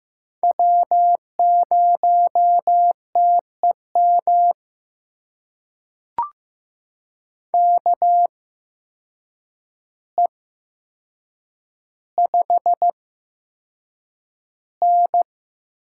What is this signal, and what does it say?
Morse code, heard as a keyed tone, about 15 wpm, spaced as for 7 wpm, 700 Hz. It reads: W0TEM KE5N
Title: USA Call Signs - 8x Character Spacing - 15wpm